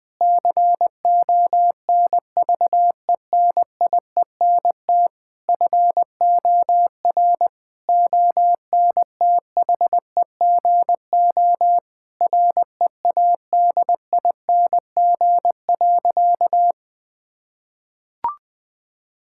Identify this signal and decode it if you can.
Morse code, heard as a keyed tone, about 20 wpm, 700 Hz. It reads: CONVENIENT FOR ONTHEGO READING.
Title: Difficult Sentences with Hints - 20wpm